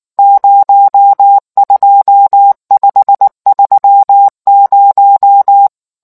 Morse code, heard as a keyed tone, about 19 wpm, 800 Hz. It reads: 02530